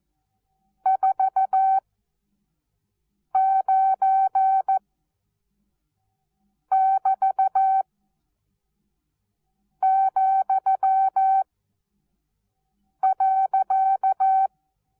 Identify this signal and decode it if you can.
Morse code, heard as a keyed tone, about 14 wpm, spaced as for 4 wpm, 770 Hz. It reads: 49<BT>,.
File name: Onlinelehrgang / morsen / DJ4UF Morsen19